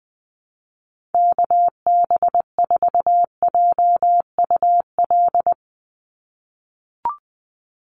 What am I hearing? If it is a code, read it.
Morse code, heard as a keyed tone, about 20 wpm, 700 Hz. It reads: KB4JUL
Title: USA Call Signs 20wpm